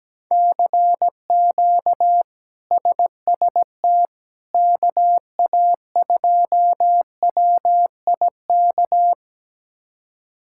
Morse code, heard as a keyed tone, about 17 wpm, 700 Hz. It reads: CQ SST KA2WIK
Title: K1USN SST QSO Practice - 17wpm